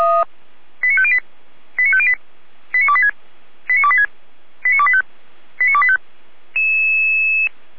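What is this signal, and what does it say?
Five-tone selective call (ZVEI), heard as quick runs of five tones, tones of about 70 ms. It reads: 89508 89508 89286 89286 89285 89285
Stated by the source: ZVEI five-tone call 89280 received on SDR.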